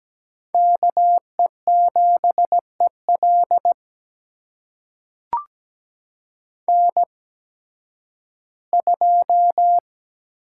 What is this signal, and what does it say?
Morse code, heard as a keyed tone, about 17 wpm, spaced as for 11 wpm, 700 Hz. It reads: KE7EL N2
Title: USA Call Signs - 5 Characters - 8x Character Spacing - 17wpm